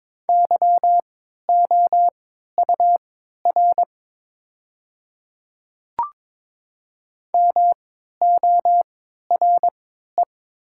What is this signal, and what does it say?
Morse code, heard as a keyed tone, about 21 wpm, spaced as for 12 wpm, 700 Hz. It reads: YOUR MORE
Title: 3x Character Spacing - Top 200 Words - 22wpm